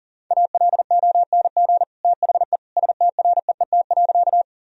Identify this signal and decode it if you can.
Morse code, heard as a keyed tone, about 40 wpm, 700 Hz. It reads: ALONG THE STREET.
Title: Sentences from Top 1000 Words - Review Part 3 - 40wpm